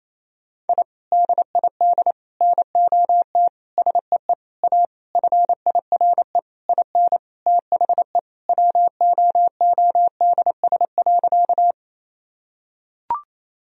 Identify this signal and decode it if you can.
Morse code, heard as a keyed tone, about 28 wpm, 700 Hz. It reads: I DID NOT SEE A FIRE IN THE WOODS.
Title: Sentences from Top 1000 Words - Review Part 2 - 28wpm